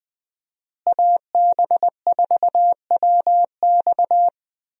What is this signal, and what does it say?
Morse code, heard as a keyed tone, about 20 wpm, 700 Hz. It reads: AB4WX
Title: USA Call Signs - 5 Characters - Repeated 3x - 20wpm